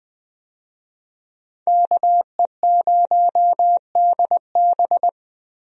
Morse code, heard as a keyed tone, about 20 wpm, 700 Hz. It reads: KE0DB
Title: USA Call Signs - 5 Characters - Repeated 3x - 20wpm